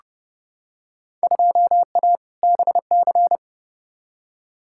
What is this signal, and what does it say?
Morse code, heard as a keyed tone, about 30 wpm, 700 Hz. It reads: 2A BC
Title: Field Day Exchanges - 30wpm